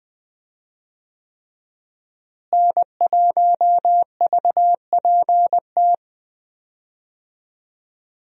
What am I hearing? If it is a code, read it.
Morse code, heard as a keyed tone, about 20 wpm, 700 Hz. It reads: N1VPT
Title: USA Call Signs - 5 Characters - Repeated 3x - 20wpm